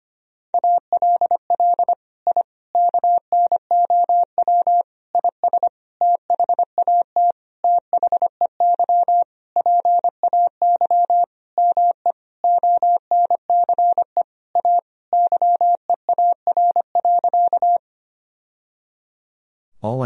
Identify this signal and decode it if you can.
Morse code, heard as a keyed tone, about 25 wpm, 700 Hz. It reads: ALL I KNOW IS THAT THEY PAY ME ONCE A YEAR.
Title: Sentences from Top 1000 Words - Review Part 1 - 25wpm